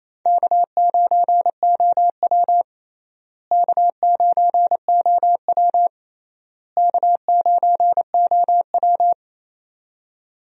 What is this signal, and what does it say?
Morse code, heard as a keyed tone, about 28 wpm, 700 Hz. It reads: K9OW K9OW K9OW
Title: USA Call Signs - 4 Characters - Repeated 3x - 28wpm